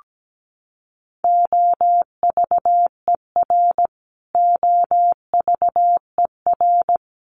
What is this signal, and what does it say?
Morse code, heard as a keyed tone, about 17 wpm, 700 Hz. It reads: OVER OVER
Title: Top 300 Words Repeated 17wpm